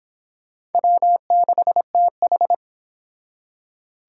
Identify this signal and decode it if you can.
Morse code, heard as a keyed tone, about 30 wpm, 700 Hz. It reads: W6TH